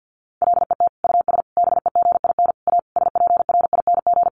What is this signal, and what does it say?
Morse code, heard as a keyed tone, about 50 wpm, 700 Hz. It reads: LET US BEGIN A SPRING
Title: Sentences from Top 1400 Words 50wpm